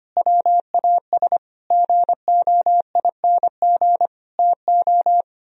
Morse code, heard as a keyed tone, about 29 wpm, 700 Hz. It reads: WAS GOING TO
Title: Speed-Racing - Sets of 5 Words 25wpm